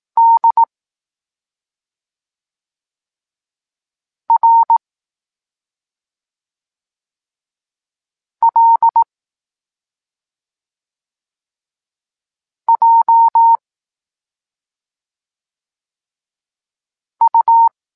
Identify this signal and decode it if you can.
Morse code, heard as a keyed tone, about 18 wpm, 920 Hz. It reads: D R L J U
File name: Tekst-26 1